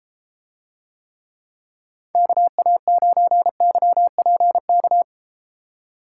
Morse code, heard as a keyed tone, about 32 wpm, 700 Hz. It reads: KA9YPK